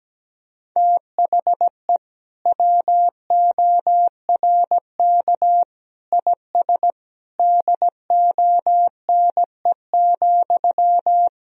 Morse code, heard as a keyed tone, about 17 wpm, 700 Hz. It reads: THE WORK IS DONE,
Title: Sentences from Top 1000 Words - Review Part 4 - 17wpm